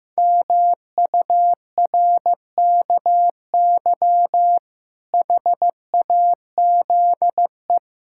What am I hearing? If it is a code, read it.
Morse code, heard as a keyed tone, about 16 wpm, 700 Hz. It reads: MURKY HAZE
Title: Pangram Challenge v2 - 15wpm